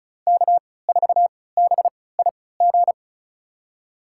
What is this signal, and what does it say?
Morse code, heard as a keyed tone, about 34 wpm, spaced as for 20 wpm, 700 Hz. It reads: K4BIG
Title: USA Call Signs - 5 Characters - 3x Character Spacing - 35wpm